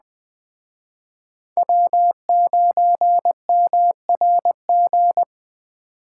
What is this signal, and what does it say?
Morse code, heard as a keyed tone, about 19 wpm, 700 Hz. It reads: W9MRG